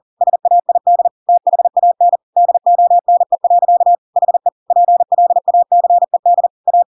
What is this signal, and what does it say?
Morse code, heard as a keyed tone, about 40 wpm, 700 Hz. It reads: SAID THAN DONE. HE PLACED A
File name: Pinocchio / Pinocchio ch5 40 40 ik1hge